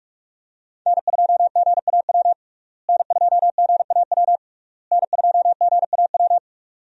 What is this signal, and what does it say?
Morse code, heard as a keyed tone, about 46 wpm, 700 Hz. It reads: N2GAW N2GAW N2GAW
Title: USA Call Signs - 5 Characters - Repeated 3x - 45wpm